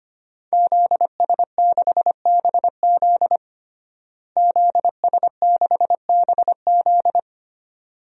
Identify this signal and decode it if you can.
Morse code, heard as a keyed tone, about 25 wpm, 700 Hz. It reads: ZS6BZ ZS6BZ
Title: DXCC-100 Call Signs - 5 Characters - Repeated 2x - 25wpm